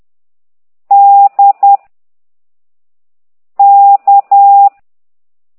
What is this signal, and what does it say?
Morse code, heard as a keyed tone, about 10 wpm, 800 Hz. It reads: D K
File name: ct2idw / exerc4